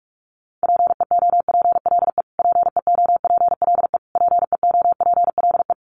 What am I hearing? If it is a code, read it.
Morse code, heard as a keyed tone, about 44 wpm, 700 Hz. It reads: PEOPLE PEOPLE PEOPLE
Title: Top 200 Words Repeated 45wpm